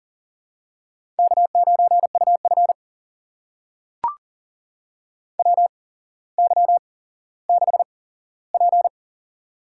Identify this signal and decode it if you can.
Morse code, heard as a keyed tone, about 40 wpm, spaced as for 17 wpm, 700 Hz. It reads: K9UF WY6P
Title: USA Call Signs - 4 Characters - 8x Character Spacing - 40wpm